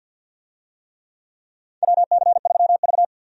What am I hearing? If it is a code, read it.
Morse code, heard as a keyed tone, about 51 wpm, 700 Hz. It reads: WC3V
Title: USA Call Signs - 4 Characters - Repeated 3x - 50wpm